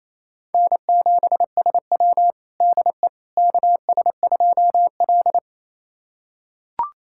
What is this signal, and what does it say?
Morse code, heard as a keyed tone, about 28 wpm, 700 Hz. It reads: N7SW DE KS2L
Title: USA Callsign DE Callsign - 4 Characters - 28wpm